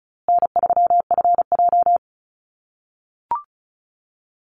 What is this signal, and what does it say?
Morse code, heard as a keyed tone, about 34 wpm, 700 Hz. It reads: N3FJ
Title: USA Call Signs Repeated 3x 35wpm